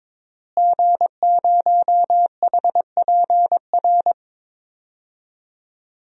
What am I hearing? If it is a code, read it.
Morse code, heard as a keyed tone, about 22 wpm, 700 Hz. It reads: G0HPR
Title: DXCC-100 Call Signs - 5 Characters - Repeated 3x - 22wpm